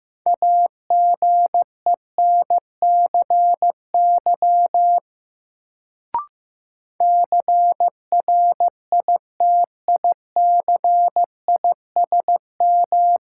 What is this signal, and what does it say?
Morse code, heard as a keyed tone, about 15 wpm, 700 Hz. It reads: AGENCY CRITICISM
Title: Sentences from Top 1300 Words - New Word Drill Alt - 15wpm